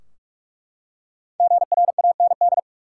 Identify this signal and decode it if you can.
Morse code, heard as a keyed tone, about 47 wpm, 700 Hz. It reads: GRAND
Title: Five Letter Words - 45wpm